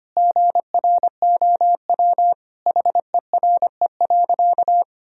Morse code, heard as a keyed tone, about 25 wpm, 700 Hz. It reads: GROW HERE.
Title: Sentences from Top 1000 Words - Review Part 1 - 25wpm